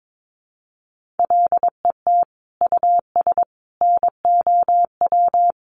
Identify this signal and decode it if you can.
Morse code, heard as a keyed tone, about 22 wpm, 700 Hz. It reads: LET US NOW
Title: Sentences from Top 1500 Words 22wpm